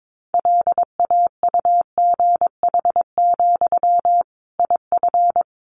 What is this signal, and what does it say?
Morse code, heard as a keyed tone, about 22 wpm, 700 Hz. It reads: LAUGH, IF
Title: Sentences from Top 500 Words 22wpm